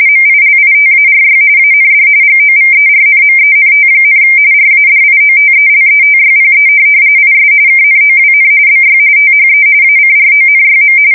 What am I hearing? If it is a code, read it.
RTTY, about 45.45 baud. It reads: MILITARY AND GOVERNMENT, RADIOTELETYPE IS KNOWN BY THE ACRONYM
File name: fsk170 / teletype-RTTY